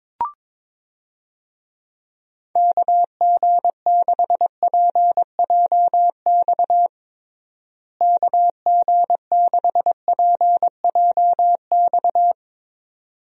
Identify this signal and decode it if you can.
Morse code, heard as a keyed tone, about 22 wpm, 700 Hz. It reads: KG6PJX KG6PJX
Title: USA Call Signs - 6 Characters - Repeated 2x - 22wpm